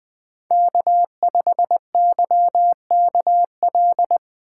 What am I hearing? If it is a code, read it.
Morse code, heard as a keyed tone, about 21 wpm, 700 Hz. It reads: K5YKL
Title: USA Call Signs - 5 Characters - 4x Character Spacing - 20wpm